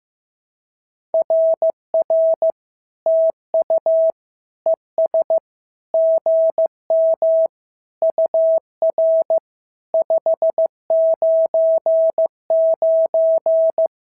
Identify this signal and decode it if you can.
Morse code, heard as a keyed tone, about 15 wpm, 650 Hz. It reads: RR TU ES GM UR 599
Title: SOTA QSO Practice 15wpm